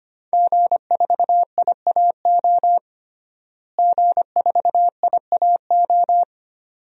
Morse code, heard as a keyed tone, about 25 wpm, 700 Hz. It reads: G4IAO G4IAO
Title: DXCC-100 Call Signs - 5 Characters - Repeated 2x - 25wpm